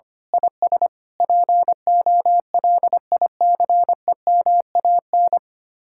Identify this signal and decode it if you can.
Morse code, heard as a keyed tone, about 24 wpm, 700 Hz. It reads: IS POLICEMAN
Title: Speed-Racing Common QSO Phrases - 25wpm